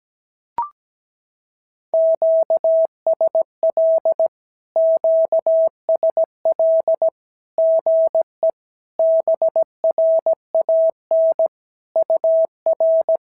Morse code, heard as a keyed tone, about 17 wpm, 650 Hz. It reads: QSL QSL GE BRAN UR
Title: SOTA QSO Practice 17wpm